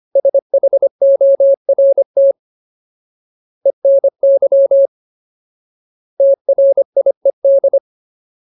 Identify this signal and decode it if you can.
Morse code, heard as a keyed tone, about 25 wpm, 550 Hz. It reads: SHORT ENY TRIED
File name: Ultimate Word List / 25WPM Ultimate